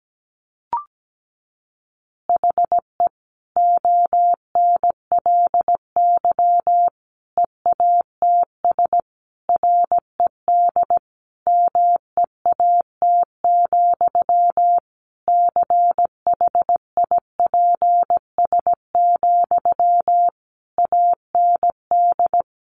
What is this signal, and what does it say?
Morse code, heard as a keyed tone, about 17 wpm, 700 Hz. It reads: HE ONLY EATS RED MEAT, CHIPS, AND
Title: Easy Phrases 17wpm